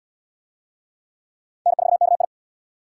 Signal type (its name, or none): Morse code